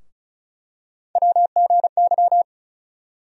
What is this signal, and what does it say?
Morse code, heard as a keyed tone, about 36 wpm, 700 Hz. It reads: WGY
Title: USA Call Sign Postfixes - 3 Letters 35wpm